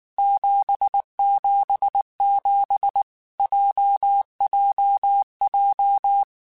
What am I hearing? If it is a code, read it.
Morse code, heard as a keyed tone, about 19 wpm, 800 Hz. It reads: 777 JJJ
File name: Basic Sets / Basic Letter Groups Collective7 / JZ17@18